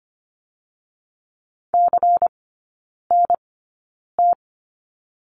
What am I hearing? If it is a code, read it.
Morse code, heard as a keyed tone, about 25 wpm, spaced as for 9 wpm, 700 Hz. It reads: CNT
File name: course / 25:10wpm / 093 Lesson 16e - Letter c - Three Characters 25wpm